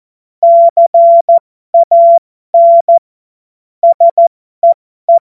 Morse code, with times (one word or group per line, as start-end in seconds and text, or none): 0.4-3.0 CAN
3.8-5.2 SEE